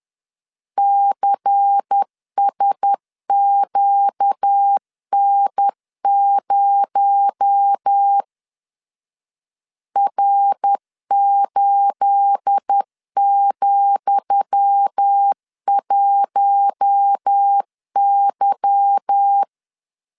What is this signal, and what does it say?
Morse code, heard as a keyed tone, about 10 wpm, 800 Hz. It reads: CSQN0 R8,1Y
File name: LEZIONI / 10 SIGNS / 10D COMA 10wpm 3013